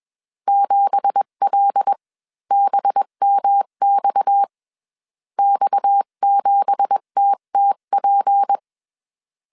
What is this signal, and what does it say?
Morse code, heard as a keyed tone, about 21 wpm, spaced as for 18 wpm, 800 Hz. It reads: 7LBMX X7TTP